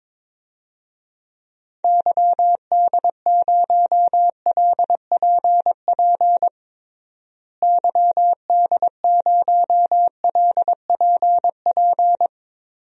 Morse code, heard as a keyed tone, about 22 wpm, 700 Hz. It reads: YD0LPP YD0LPP
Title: DXCC-100 Call Signs - 6 Characters - Repeated 2x - 22wpm